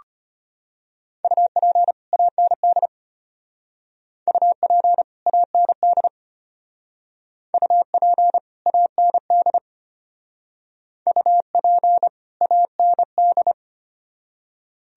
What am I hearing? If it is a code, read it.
Morse code, heard as a keyed tone, about 31 wpm, 700 Hz. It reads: UP AND UP AND UP AND UP AND
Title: Speed-Racing - Sets of 2 Words 25wpm